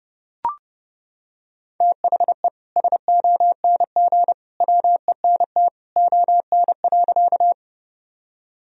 Morse code, heard as a keyed tone, about 32 wpm, 700 Hz. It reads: THE SONG WENT ON.
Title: Sentences from Top 1000 Words - Review Part 2 - 30wpm